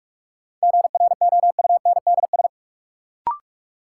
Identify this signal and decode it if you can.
Morse code, heard as a keyed tone, about 46 wpm, 700 Hz. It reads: GROUNDS
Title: Sentences from Top 1300 Words - New Word Drill - 45wpm